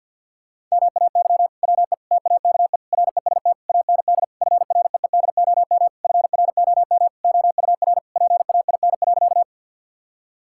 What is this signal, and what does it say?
Morse code, heard as a keyed tone, about 50 wpm, 700 Hz. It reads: MAY WE TAKE REST AND FREEDOM FROM OUR PAIN.
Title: Sentences from Top 1600 Words 50wpm